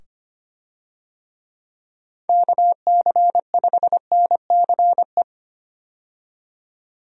Morse code, heard as a keyed tone, about 25 wpm, 700 Hz. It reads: KC5NCE
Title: USA Call Signs - 6 Characters - 25wpm